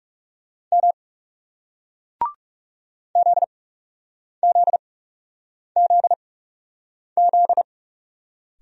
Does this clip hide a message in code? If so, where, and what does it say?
Yes: Morse code from 0.7 to 7.6 s, heard as a keyed tone, about 38 wpm, spaced as for 8 wpm, 700 Hz. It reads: M ZZZZ